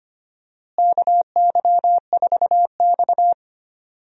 Morse code, heard as a keyed tone, about 25 wpm, 700 Hz. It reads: KY4X
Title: USA Call Signs - 4 Characters - Repeated 3x - 25wpm